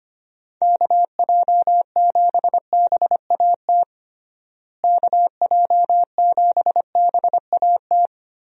Morse code, heard as a keyed tone, about 25 wpm, 700 Hz. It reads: KJ7BAT KJ7BAT
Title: USA Call Signs - 6 Characters - Repeated 3x - 25wpm